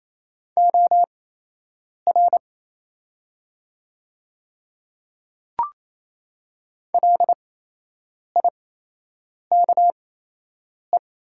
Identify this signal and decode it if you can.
Morse code, heard as a keyed tone, about 28 wpm, spaced as for 8 wpm, 700 Hz. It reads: OR LIKE
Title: 8x Character Spacing - Top 100 Words - 28wpm